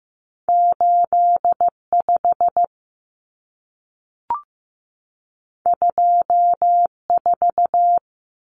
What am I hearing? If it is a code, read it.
Morse code, heard as a keyed tone, about 15 wpm, 700 Hz. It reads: 85 24
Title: Common Cut Numbers - 2-Digit 15wpm